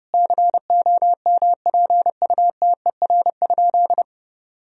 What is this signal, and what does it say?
Morse code, heard as a keyed tone, about 32 wpm, 700 Hz. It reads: COMPUTER?
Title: Sentences from Top 1000 Words - Review Part 3 - 30wpm